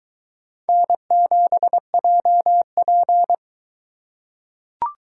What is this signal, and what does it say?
Morse code, heard as a keyed tone, about 22 wpm, 700 Hz. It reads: N7JP